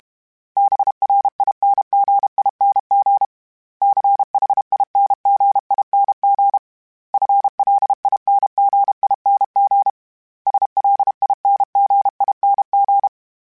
Morse code, heard as a keyed tone, about 32 wpm, 800 Hz. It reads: BRINGING CHINGING FLINGING SLINGING
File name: Special INGs-mp3s / Special INGs-mp3s / Dble ings @30